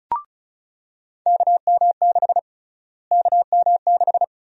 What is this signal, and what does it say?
Morse code, heard as a keyed tone, about 34 wpm, 700 Hz. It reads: KM6 KM6
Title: USA Call Sign Prefixes - Repeated 2x - 35wpm